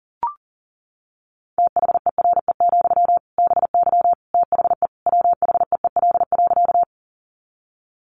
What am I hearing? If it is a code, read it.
Morse code, heard as a keyed tone, about 40 wpm, 700 Hz. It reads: THERE, BY THE WHEEL.
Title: Sentences from Top 500 Words 40wpm